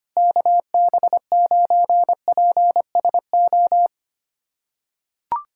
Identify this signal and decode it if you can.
Morse code, heard as a keyed tone, about 25 wpm, 700 Hz. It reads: KB9PSO
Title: USA Callsign DE Callsign 25wpm